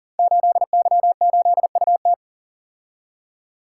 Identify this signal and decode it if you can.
Morse code, heard as a keyed tone, about 42 wpm, 700 Hz. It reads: 8Y8UT
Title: Code Groups - 5 Letter:Numbers - 40wpm